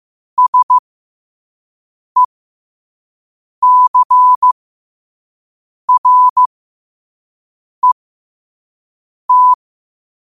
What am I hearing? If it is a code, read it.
Morse code, heard as a keyed tone, about 15 wpm, spaced as for 5 wpm, 1000 Hz. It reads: SECRET
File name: qotdfiles / 5wpm